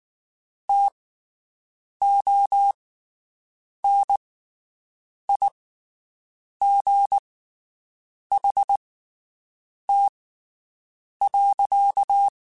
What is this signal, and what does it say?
Morse code, heard as a keyed tone, about 19 wpm, spaced as for 7 wpm, 800 Hz. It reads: TONIGHT.
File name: (spaced) / QSOs Spaced-mp3s / WD0BGP 1 @18 sp